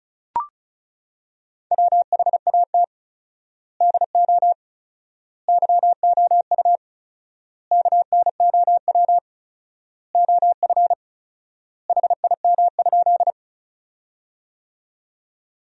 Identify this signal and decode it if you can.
Morse code, heard as a keyed tone, about 34 wpm, 700 Hz. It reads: WHAT DO YOU KNOW OF HIM?
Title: 4x Word-Spacing - Sentences from Top 100 Words - 35wpm